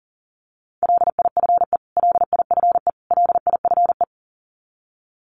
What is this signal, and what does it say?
Morse code, heard as a keyed tone, about 37 wpm, 700 Hz. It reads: LIFE LIFE LIFE
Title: Top 500 Words Repeated 40wpm